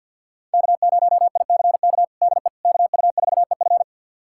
Morse code, heard as a keyed tone, about 51 wpm, 700 Hz. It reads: K0ICK DE KU4EF
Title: USA Callsign DE Callsign - 5 Characters - 50wpm